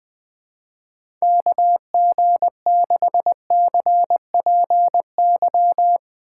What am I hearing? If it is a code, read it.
Morse code, heard as a keyed tone, about 20 wpm, 700 Hz. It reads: KG6CPY
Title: USA Call Signs - 6 Characters - 4x Character Spacing - 20wpm